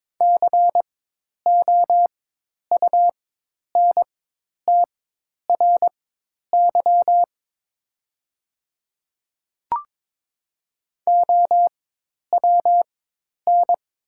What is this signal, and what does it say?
Morse code, heard as a keyed tone, about 22 wpm, spaced as for 10 wpm, 700 Hz. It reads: COUNTRY OWN
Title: 4x Character Spacing - Top 300 Words - 22wpm